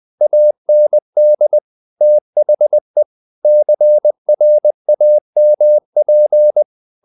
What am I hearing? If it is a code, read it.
Morse code, heard as a keyed tone, about 20 wpm, 600 Hz. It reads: AND THE CRAMP
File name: the old man and the sea part 2 20wpm